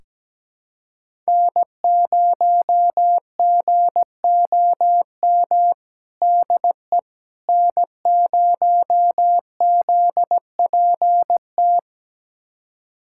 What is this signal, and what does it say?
Morse code, heard as a keyed tone, about 17 wpm, 700 Hz. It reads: N0GOM DE N0ZPT
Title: USA Callsign DE Callsign - 5 Characters - 17wpm